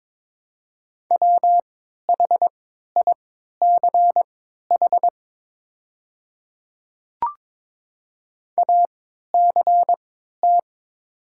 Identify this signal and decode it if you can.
Morse code, heard as a keyed tone, about 23 wpm, spaced as for 13 wpm, 700 Hz. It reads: WHICH ACT
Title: 3x Character Spacing - Top 200 Words - 22wpm